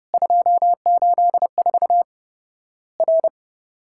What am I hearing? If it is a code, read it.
Morse code, heard as a keyed tone, about 30 wpm, 700 Hz. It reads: 284 R
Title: Worked All Europe DX Contest - QTC Exchange - 30wpm